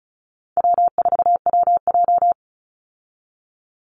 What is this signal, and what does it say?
Morse code, heard as a keyed tone, about 34 wpm, 700 Hz. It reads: W4WJ
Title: Easy Call Signs Repeated 3x - 35wpm